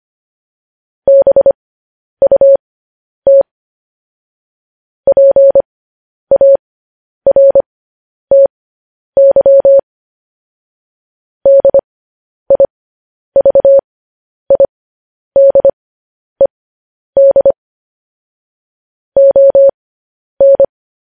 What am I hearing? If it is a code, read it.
Morse code, heard as a keyed tone, about 25 wpm, spaced as for 10 wpm, 550 Hz. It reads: BUT PARTY DIVIDED ON
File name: podcast / News headlines-1x-25x10